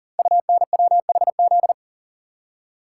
Morse code, heard as a keyed tone, about 40 wpm, 700 Hz. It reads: UNWHZ
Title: Code Groups - 5 Letters - 40wpm